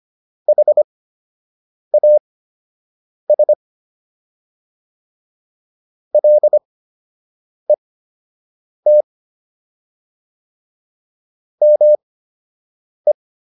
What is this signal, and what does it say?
Morse code, heard as a keyed tone, about 25 wpm, spaced as for 7 wpm, 600 Hz. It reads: HAS LET ME